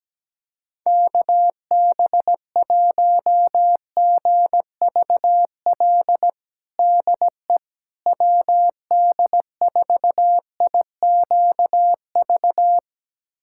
Morse code, heard as a keyed tone, about 17 wpm, 700 Hz. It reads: KB1GVL DE WD4IQV
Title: USA Callsign DE Callsign - 6 Characters - 17wpm